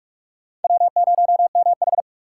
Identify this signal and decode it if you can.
Morse code, heard as a keyed tone, about 37 wpm, 700 Hz. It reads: W0MH
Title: Speed Racing - USA 1x2 Call Signs 30wpm